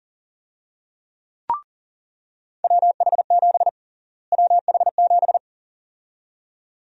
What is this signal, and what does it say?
Morse code, heard as a keyed tone, about 40 wpm, 700 Hz. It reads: WH7 WH7
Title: USA Call Sign Prefixes - Repeated 2x - 40wpm